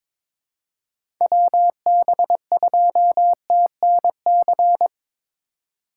Morse code, heard as a keyed tone, about 21 wpm, 700 Hz. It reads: WB2TNC
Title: USA Call Signs - 6 Characters - 2x Character Spacing - 22wpm